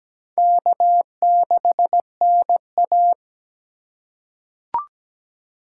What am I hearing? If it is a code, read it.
Morse code, heard as a keyed tone, about 17 wpm, 700 Hz. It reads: K6NA
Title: K1USN SST QSO Practice - 17wpm